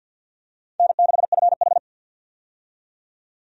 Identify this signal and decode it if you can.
Morse code, heard as a keyed tone, about 52 wpm, 700 Hz. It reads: N6FH
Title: USA Call Signs - 4 Characters - 50wpm